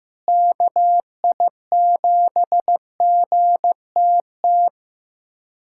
Morse code, heard as a keyed tone, about 15 wpm, 700 Hz. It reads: KI7GTT
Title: USA Callsign DE Callsign 15wpm